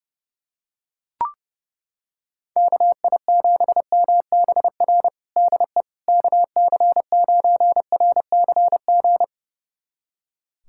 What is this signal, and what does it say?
Morse code, heard as a keyed tone, about 30 wpm, 700 Hz. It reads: KI7MBR DE KC9RCG
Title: USA Callsign DE Callsign - 6 Characters - 30wpm